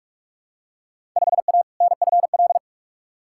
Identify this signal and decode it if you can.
Morse code, heard as a keyed tone, about 46 wpm, 700 Hz. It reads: 5A NFL
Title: Field Day Exchanges - 45wpm